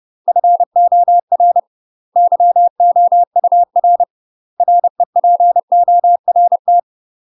Morse code, heard as a keyed tone, about 30 wpm, 700 Hz. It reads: FOR YOUR REPORT
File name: UK MASTER QSOs TRAINING / 30WPM / UK MASTER QSOs 0022